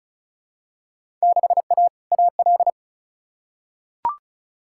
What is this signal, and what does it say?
Morse code, heard as a keyed tone, about 36 wpm, 690 Hz. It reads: 6A AL
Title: Field Day Exchanges Encore - 35wpm